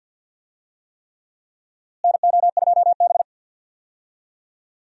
Morse code, heard as a keyed tone, about 52 wpm, 700 Hz. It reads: NO2B
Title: USA Call Signs - 4 Characters - Repeated 3x - 50wpm